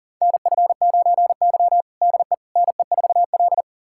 Morse code, heard as a keyed tone, about 40 wpm, 700 Hz. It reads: NF9Y DE NE4L
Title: USA Callsign DE Callsign - 4 Characters - 40wpm